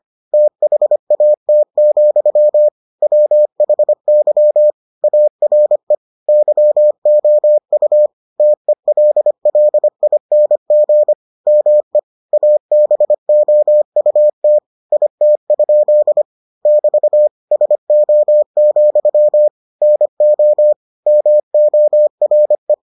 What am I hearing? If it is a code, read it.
Morse code, heard as a keyed tone, about 25 wpm, 600 Hz. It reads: THAT, WHY ARE YOU TELLING ME ABOUT IT? <BT> SO, NO MORE